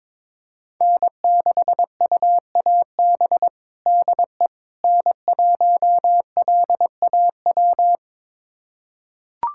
Morse code, heard as a keyed tone, about 22 wpm, 700 Hz. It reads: N6UAB DE N1LAW
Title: USA Callsign DE Callsign - 5 Characters - 22wpm